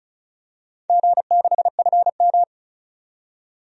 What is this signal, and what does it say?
Morse code, heard as a keyed tone, about 34 wpm, 700 Hz. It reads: G6FM
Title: DXCC-100 Call Signs - 4 Characters - 35wpm